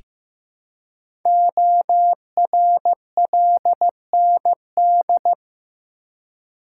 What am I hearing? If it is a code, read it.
Morse code, heard as a keyed tone, about 15 wpm, 700 Hz. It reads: ORLND
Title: Code Groups - 5 Letters - 15wpm